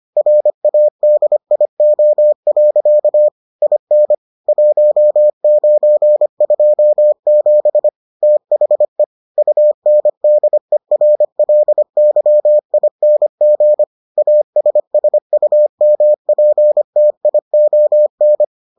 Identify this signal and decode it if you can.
Morse code, heard as a keyed tone, about 25 wpm, 600 Hz. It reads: RADIO. IN 1927 THE UNDERLYING ASSUMPTION